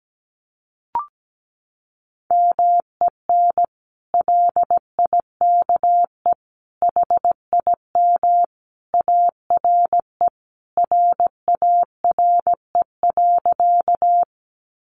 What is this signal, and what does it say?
Morse code, heard as a keyed tone, about 17 wpm, 700 Hz. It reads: MEN LIKE HIM ARE RARE.
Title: Sentences from Top 1100 Words 17wpm